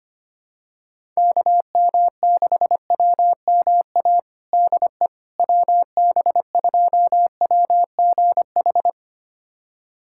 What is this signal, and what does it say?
Morse code, heard as a keyed tone, about 25 wpm, 700 Hz. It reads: KM6WMA DE WB2WGH
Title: USA Callsign DE Callsign - 6 Characters - 25wpm